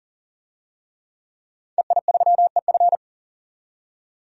Morse code, heard as a keyed tone, about 40 wpm, 700 Hz. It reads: EI3EF
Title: DXCC-100 Call Signs - 5 Characters - 40wpm